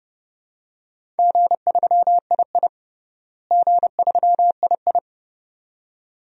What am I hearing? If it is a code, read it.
Morse code, heard as a keyed tone, about 30 wpm, 700 Hz. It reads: G3II G3II
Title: DXCC-100 Call Signs - 4 Characters - Repeated 2x - 30wpm